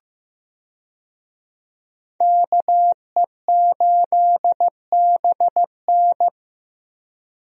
Morse code, heard as a keyed tone, about 15 wpm, 700 Hz. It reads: KE8BN
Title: USA Call Signs - 5 Characters - Repeated 3x - 15wpm